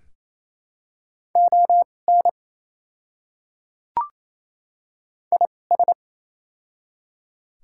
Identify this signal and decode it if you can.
Morse code, heard as a keyed tone, about 27 wpm, spaced as for 20 wpm, 690 Hz. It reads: ON IS
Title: 2x Character Spacing - Top 100 Words - 28wpm